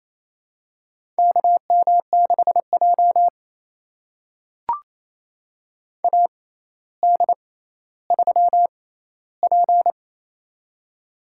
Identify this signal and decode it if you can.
Morse code, heard as a keyed tone, about 28 wpm, spaced as for 14 wpm, 700 Hz. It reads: KM6J AD3P
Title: USA Call Signs - 4 Characters - 6x Character Spacing - 28wpm